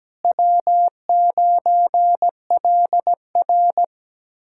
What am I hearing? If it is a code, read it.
Morse code, heard as a keyed tone, about 17 wpm, 700 Hz. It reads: W9LR
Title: K1USN SST QSO Practice - 17wpm